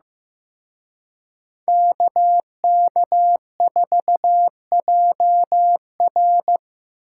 Morse code, heard as a keyed tone, about 18 wpm, 700 Hz. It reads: KK4JR
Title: Speed Racing - USA 2x2 Call Signs 15wpm